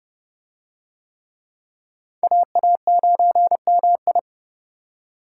Morse code, heard as a keyed tone, about 27 wpm, 700 Hz. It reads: AA9MI